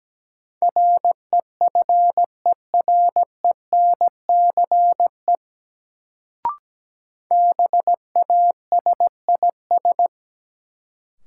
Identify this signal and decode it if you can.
Morse code, heard as a keyed tone, about 17 wpm, 700 Hz. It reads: REFERENCE BASIS
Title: Sentences from Top 1200 Words - New Word Drill Alt - 17wpm